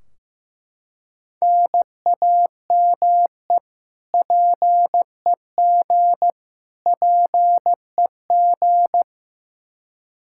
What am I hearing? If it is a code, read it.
Morse code, heard as a keyed tone, about 15 wpm, 700 Hz. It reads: NAME PEG PEG
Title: Practice QSO - Exchange 1 - Beginning - 15wpm